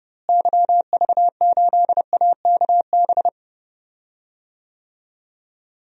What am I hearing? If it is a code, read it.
Morse code, heard as a keyed tone, about 29 wpm, 700 Hz. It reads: YV8AKB
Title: DXCC-100 Call Signs - 6 Characters - Repeated 3x - 30wpm